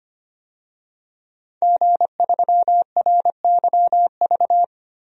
Morse code, heard as a keyed tone, about 25 wpm, 700 Hz. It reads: G3RYV
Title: DXCC-100 Call Signs - 5 Characters - Repeated 2x - 25wpm